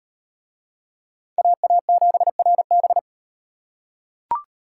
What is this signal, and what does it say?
Morse code, heard as a keyed tone, about 38 wpm, 700 Hz. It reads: AA7RB